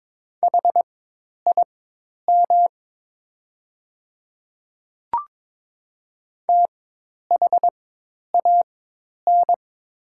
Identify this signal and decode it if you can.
Morse code, heard as a keyed tone, about 22 wpm, spaced as for 10 wpm, 700 Hz. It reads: HIM THAN
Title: 4x Character Spacing - Top 100 Words - 22wpm